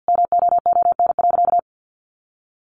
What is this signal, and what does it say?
Morse code, heard as a keyed tone, about 50 wpm, 700 Hz. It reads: MOON.
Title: A Sound Of Thunder 50wpm